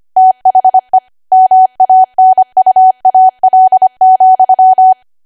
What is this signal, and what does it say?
Morse code, heard as a keyed tone, about 25 wpm, 750 Hz. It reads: THE MANUAL,
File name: Archive / 25 WPM / 25WPM